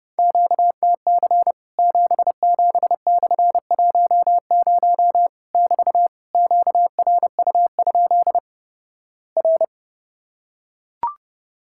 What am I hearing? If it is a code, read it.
Morse code, heard as a keyed tone, about 31 wpm, 700 Hz. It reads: QTC 77/10 <BT> QRU? R
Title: Worked All Europe DX Contest - QTC Exchange - 30wpm